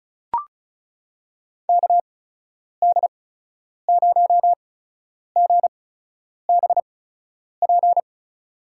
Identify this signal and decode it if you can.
Morse code, heard as a keyed tone, about 36 wpm, spaced as for 10 wpm, 700 Hz. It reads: KD0GBP
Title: USA Call Signs - 8x Character Spacing - 35wpm